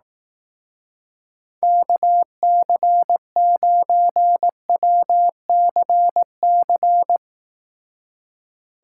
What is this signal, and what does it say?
Morse code, heard as a keyed tone, about 18 wpm, 700 Hz. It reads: KC9WCC
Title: Speed Racing - USA 2x3 Call Signs 15wpm